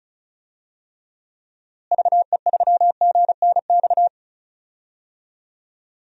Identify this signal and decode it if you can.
Morse code, heard as a keyed tone, about 36 wpm, 700 Hz. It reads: VE3GNX
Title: DXCC-100 Call Signs - 6 Characters - Repeated 2x - 35wpm